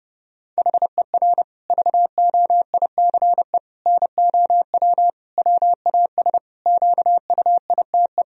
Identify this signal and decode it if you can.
Morse code, heard as a keyed tone, about 30 wpm, 700 Hz. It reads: HER VOICE NOW WAS QUITE
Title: Sentences from Top 1000 Words - Review Part 3 - 30wpm